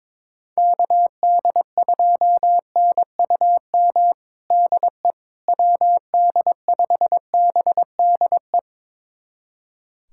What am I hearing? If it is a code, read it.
Morse code, heard as a keyed tone, about 22 wpm, 700 Hz. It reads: KD2NUM DE WD5BDE
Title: USA Callsign DE Callsign 22wpm